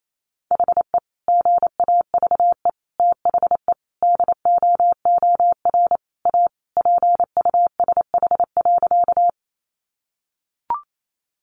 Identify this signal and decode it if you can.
Morse code, heard as a keyed tone, about 28 wpm, 700 Hz. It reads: HE GAVE THE DOOR A PUSH.
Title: Sentences from Top 1300 Words 28wpm